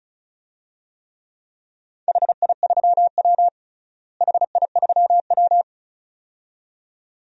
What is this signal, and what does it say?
Morse code, heard as a keyed tone, about 34 wpm, 700 Hz. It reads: HI3W HI3W
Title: DXCC-100 Call Signs - 4 Characters - Repeated 2x - 35wpm